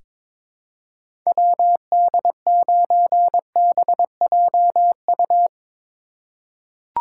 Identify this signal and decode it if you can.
Morse code, heard as a keyed tone, about 22 wpm, 700 Hz. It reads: WD9BJU
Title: USA Call Signs - 6 Characters - 2x Character Spacing - 22wpm